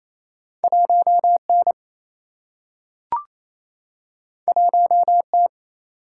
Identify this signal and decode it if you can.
Morse code, heard as a keyed tone, about 28 wpm, 700 Hz. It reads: 1 N 1 T
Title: Single-Letter Single-Number 28wpm